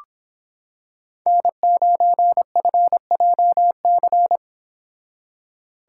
Morse code, heard as a keyed tone, about 26 wpm, 700 Hz. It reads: N9FJC